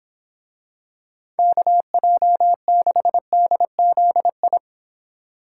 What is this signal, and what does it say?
Morse code, heard as a keyed tone, about 26 wpm, 700 Hz. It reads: KJ6DZI